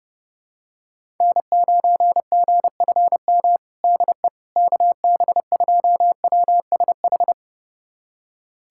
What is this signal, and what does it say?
Morse code, heard as a keyed tone, about 32 wpm, 700 Hz. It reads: N9GFM DE KB2WSH
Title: USA Callsign DE Callsign 30wpm